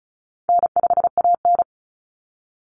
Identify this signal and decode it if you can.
Morse code, heard as a keyed tone, about 34 wpm, 700 Hz. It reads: N5AN
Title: Easy Call Signs Repeated 3x - 35wpm